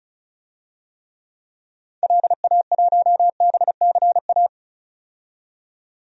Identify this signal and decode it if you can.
Morse code, heard as a keyed tone, about 36 wpm, 700 Hz. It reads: LA1BCA
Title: DXCC-100 Call Signs - 6 Characters - 35wpm